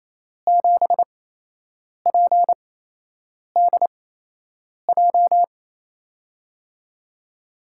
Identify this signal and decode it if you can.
Morse code, heard as a keyed tone, about 28 wpm, spaced as for 8 wpm, 700 Hz. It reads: 7PDJ
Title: USA Call Signs - 5 Characters - 8x Character Spacing - 28wpm